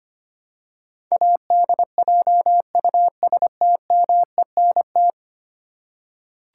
Morse code, heard as a keyed tone, about 25 wpm, 700 Hz. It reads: ADJUSTMENT
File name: Ten Letter Words - 25wpm